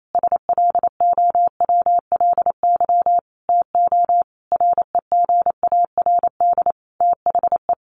Morse code, heard as a keyed tone, about 28 wpm, 700 Hz. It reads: SLOWLY TO REGARD THE
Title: A Sound Of Thunder 28wpm